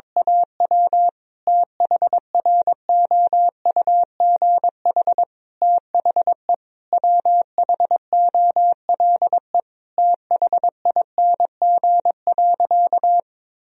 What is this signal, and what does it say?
Morse code, heard as a keyed tone, about 22 wpm, 700 Hz. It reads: AW THROUGH THE WHOLE THING.
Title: Sentences from Top 1000 Words - Review Part 1 - 22wpm